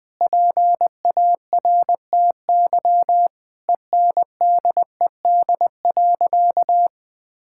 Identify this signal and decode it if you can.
Morse code, heard as a keyed tone, about 21 wpm, 700 Hz. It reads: PARTY ENDED.
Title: Sentences with 2-Word Phrases - Spoken Beginning - 20wpm